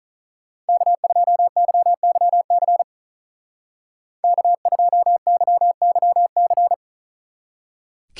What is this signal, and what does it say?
Morse code, heard as a keyed tone, about 41 wpm, 700 Hz. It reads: K2YYC K2YYC
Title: Speed Racing - USA 1x3 Call Signs 35wpm